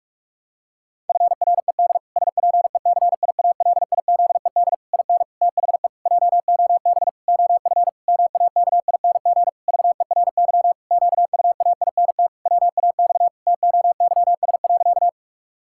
Morse code, heard as a keyed tone, about 46 wpm, 700 Hz. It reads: FRED SPECIALIZED IN THE JOB OF MAKING VERY QUAINT WAX TOYS.
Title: Pangram Challenge v2 - 45wpm